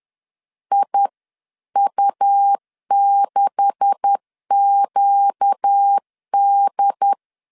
Morse code, heard as a keyed tone, about 10 wpm, 800 Hz. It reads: IU6QD